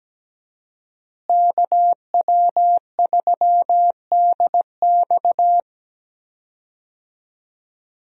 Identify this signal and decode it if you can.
Morse code, heard as a keyed tone, about 17 wpm, 700 Hz. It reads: KW3DX
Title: USA Call Signs - 5 Characters - Repeated 3x - 17wpm